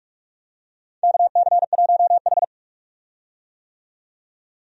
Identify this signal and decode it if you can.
Morse code, heard as a keyed tone, about 46 wpm, 700 Hz. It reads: KC1H
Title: USA Call Signs - 4 Characters - 45wpm